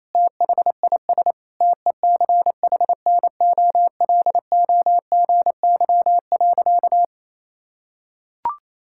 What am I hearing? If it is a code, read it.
Morse code, heard as a keyed tone, about 28 wpm, 700 Hz. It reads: THIS TECHNOLOGY.
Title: Sentences from Top 1000 Words - Review Part 5 - 28wpm